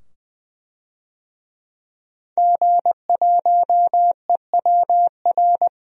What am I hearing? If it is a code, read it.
Morse code, heard as a keyed tone, about 20 wpm, 700 Hz. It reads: G1EWR
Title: DXCC-100 Call Signs - 5 Characters - 20wpm